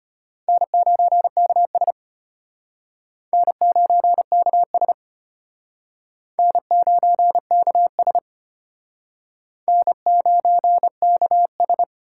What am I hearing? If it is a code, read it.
Morse code, heard as a keyed tone, about 33 wpm, 700 Hz. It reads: N9KS N9KS N9KS N9KS